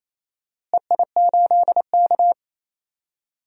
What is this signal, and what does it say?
Morse code, heard as a keyed tone, about 28 wpm, 700 Hz. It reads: EI8K
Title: DXCC-100 Call Signs - 4 Characters - Repeated 3x - 28wpm